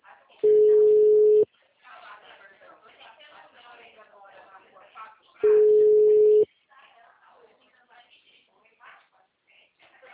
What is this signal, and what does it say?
Ringback tone (Europe, 425 Hz), with speech.